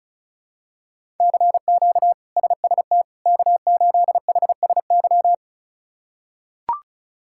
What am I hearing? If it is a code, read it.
Morse code, heard as a keyed tone, about 34 wpm, 700 Hz. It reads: CQ SST K8HSY
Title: K1USN SST QSO Practice - 35wpm